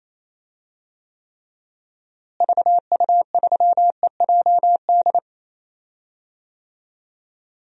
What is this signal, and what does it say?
Morse code, heard as a keyed tone, about 27 wpm, 700 Hz. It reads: VU3EJD
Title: DXCC-100 Call Signs - 6 Characters - Repeated 3x - 28wpm